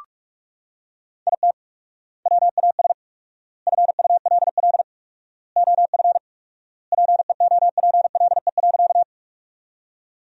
Morse code, heard as a keyed tone, about 46 wpm, 700 Hz. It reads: IT WAS FULL OF PEOPLE.
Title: 4x Word-Spacing - Sentences from Top 500 Words - 45wpm